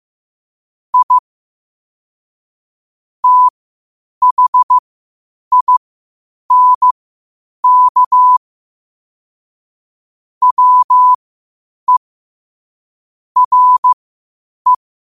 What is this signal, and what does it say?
Morse code, heard as a keyed tone, about 15 wpm, spaced as for 8 wpm, 1000 Hz. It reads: I THINK WE RE
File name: qotdfiles / 8wpm